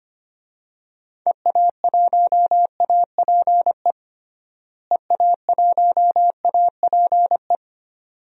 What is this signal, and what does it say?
Morse code, heard as a keyed tone, about 25 wpm, 700 Hz. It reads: EA1APE EA1APE
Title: DXCC-100 Call Signs - 6 Characters - Repeated 2x - 25wpm